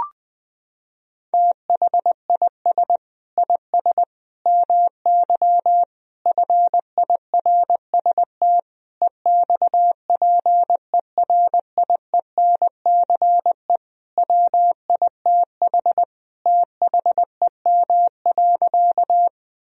Morse code, heard as a keyed tone, about 21 wpm, 700 Hz. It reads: THIS IS MY FIRST EXPERIENCE WITH THEM.
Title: Sentences from Top 1000 Words - Review Part 3 - 20wpm